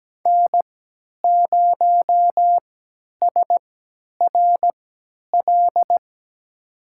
Morse code, heard as a keyed tone, about 17 wpm, spaced as for 10 wpm, 700 Hz. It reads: N0SRL
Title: USA Call Signs - 5 Characters - 3x Character Spacing - 17wpm